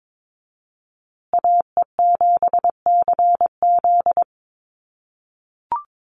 Morse code, heard as a keyed tone, about 22 wpm, 700 Hz. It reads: AE7CZ
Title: USA Call Signs Repeated 3x 22wpm